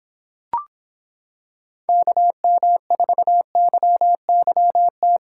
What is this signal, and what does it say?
Morse code, heard as a keyed tone, about 26 wpm, 700 Hz. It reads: KM4YYT